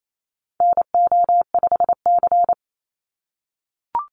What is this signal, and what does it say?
Morse code, heard as a keyed tone, about 28 wpm, 700 Hz. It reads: NO5C
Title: USA Call Signs 28wpm